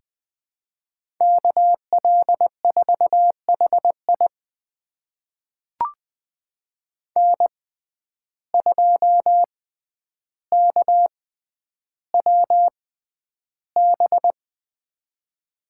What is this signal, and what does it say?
Morse code, heard as a keyed tone, about 20 wpm, spaced as for 10 wpm, 700 Hz. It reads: KL4HI N2KWB
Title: USA Call Signs - 6x Character Spacing - 20wpm